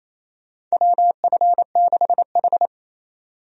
Morse code, heard as a keyed tone, about 28 wpm, 700 Hz. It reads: WF6H
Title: USA Call Signs - 4 Characters - 28wpm